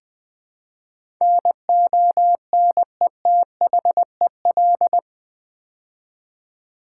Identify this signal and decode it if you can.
Morse code, heard as a keyed tone, about 20 wpm, 700 Hz. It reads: NONETHEL
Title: Build A Word - 4 Letters At A Time - 11-Letter Words - 20wpm